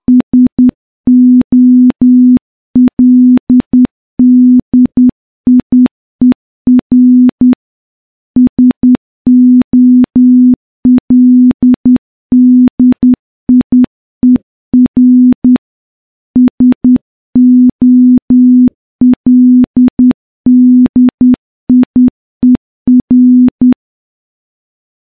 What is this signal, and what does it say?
Morse code, heard as a keyed tone, about 10 wpm, 250 Hz. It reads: SOLDIER SOLDIER SOLDIER
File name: morse-training-10WPM